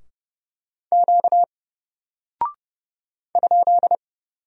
Morse code, heard as a keyed tone, about 30 wpm, 700 Hz. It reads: Q ?
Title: Single Character (Letter:Number:Punctuation:Prosign) Alt 30wpm